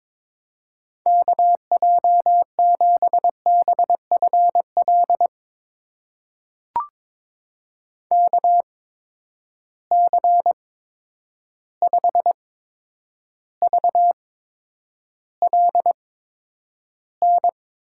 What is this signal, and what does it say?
Morse code, heard as a keyed tone, about 21 wpm, spaced as for 9 wpm, 700 Hz. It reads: KJ7BFL KC5VLN
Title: USA Call Signs - 6 Characters - 8x Character Spacing - 22wpm